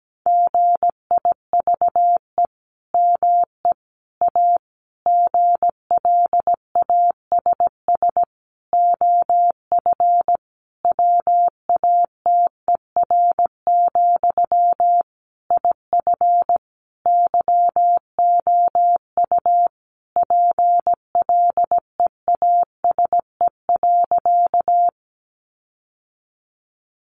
Morse code, heard as a keyed tone, about 17 wpm, 700 Hz. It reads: GIVE ME A GLASS OF WATER, IF YOU PLEASE.
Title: Sentences from Top 1500 Words 17wpm